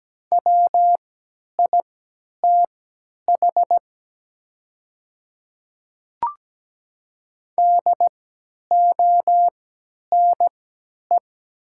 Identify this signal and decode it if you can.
Morse code, heard as a keyed tone, about 17 wpm, spaced as for 10 wpm, 700 Hz. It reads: WITH DONE
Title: 3x Character Spacing - Top 500 Words - 17wpm